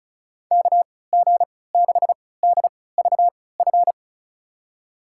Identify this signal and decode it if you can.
Morse code, heard as a keyed tone, about 36 wpm, spaced as for 20 wpm, 700 Hz. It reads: KG6DVF
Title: USA Call Signs - 6 Characters - 3x Character Spacing - 35wpm